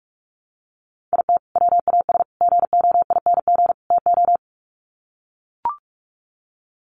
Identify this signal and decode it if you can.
Morse code, heard as a keyed tone, about 44 wpm, 700 Hz. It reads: IT WAS GOING TO
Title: Sets of 4 Words 45wpm